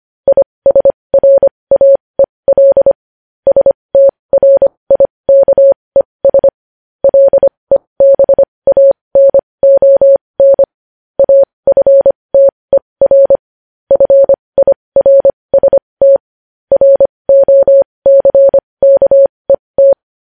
Morse code, heard as a keyed tone, about 25 wpm, spaced as for 20 wpm, 550 Hz. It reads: ISRAEL STRIKES LEBANON AFTER FIRST ROCKET
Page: news-headlines-in-morse-code-at-20-wpm